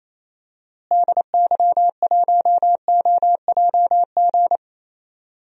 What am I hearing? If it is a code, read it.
Morse code, heard as a keyed tone, about 28 wpm, 700 Hz. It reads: DY1OJG
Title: DXCC-100 Call Signs - 6 Characters - Repeated 2x - 28wpm